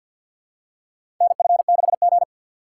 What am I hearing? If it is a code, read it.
Morse code, heard as a keyed tone, about 53 wpm, 700 Hz. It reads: NF6G
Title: USA Call Signs - 4 Characters - 3x Character Spacing - 50wpm